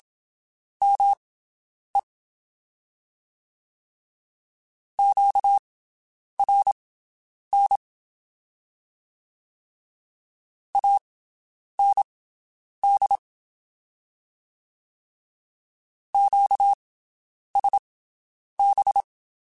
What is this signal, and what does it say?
Morse code, heard as a keyed tone, about 27 wpm, spaced as for 9 wpm, 800 Hz. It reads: ME QRN AND QSB
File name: (spaced) / QSOs Spaced-mp3s / WD0BGP 1 @25 sp